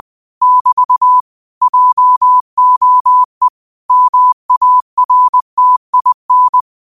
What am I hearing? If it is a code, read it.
Morse code, heard as a keyed tone, about 20 wpm, 1000 Hz. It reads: <BT> JOE MARTIN
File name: qotdfiles / 20wpm